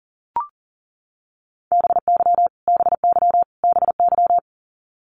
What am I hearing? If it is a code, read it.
Morse code, heard as a keyed tone, about 37 wpm, 700 Hz. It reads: BY BY BY
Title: Top 500 Words Repeated 40wpm